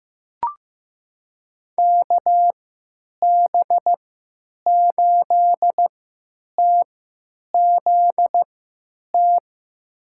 Morse code, heard as a keyed tone, about 15 wpm, spaced as for 9 wpm, 700 Hz. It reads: KB8TZT
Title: USA Call Signs - 3x Character Spacing - 15wpm